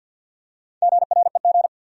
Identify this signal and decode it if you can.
Morse code, heard as a keyed tone, about 52 wpm, 700 Hz. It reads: GREG
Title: USA Common Names - Max 4 Letters - 50wpm